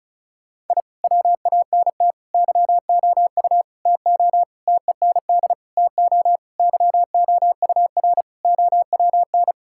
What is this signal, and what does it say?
Morse code, heard as a keyed tone, about 36 wpm, 700 Hz. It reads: I WANT YOU TO TEND TO YOUR OWN
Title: Sentences from Top 1000 Words - Review Part 5 - 35wpm